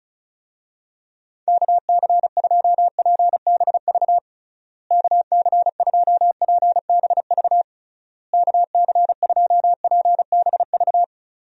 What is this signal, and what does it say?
Morse code, heard as a keyed tone, about 36 wpm, 700 Hz. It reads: KC2PBV KC2PBV KC2PBV
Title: USA Call Signs - 6 Characters - Repeated 3x - 35wpm